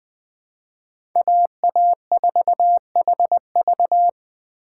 Morse code, heard as a keyed tone, about 20 wpm, 700 Hz. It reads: AA4HV
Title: K1USN SST QSO Practice - 20wpm